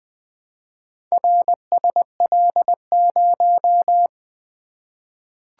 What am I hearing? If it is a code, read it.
Morse code, heard as a keyed tone, about 20 wpm, 700 Hz. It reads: RSL0
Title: Four Letter-Number Combinations - 20wpm